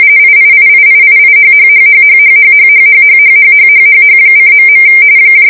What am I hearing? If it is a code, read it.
RTTY, about 45.45 baud. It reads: IIIIIIIHHHHHHHHHHHHHHHHH:..